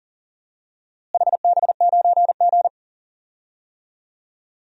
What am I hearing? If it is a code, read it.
Morse code, heard as a keyed tone, about 40 wpm, 700 Hz. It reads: HB9G
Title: DXCC-100 Call Signs - 4 Characters - Repeated 3x - 40wpm